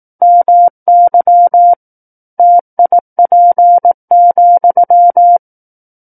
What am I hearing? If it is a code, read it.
Morse code, heard as a keyed tone, about 18 wpm, 700 Hz. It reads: MY TIP,
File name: 18 WPM